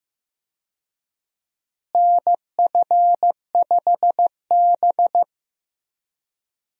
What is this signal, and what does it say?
Morse code, heard as a keyed tone, about 15 wpm, 700 Hz. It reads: NF5B
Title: USA Call Signs - 4 Characters - Repeated 2x - 15wpm